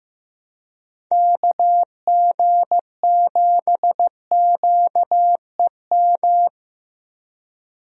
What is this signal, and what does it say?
Morse code, heard as a keyed tone, about 15 wpm, 700 Hz. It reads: KG7QEM
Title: USA Call Signs - 6 Characters - Repeated 3x - 15wpm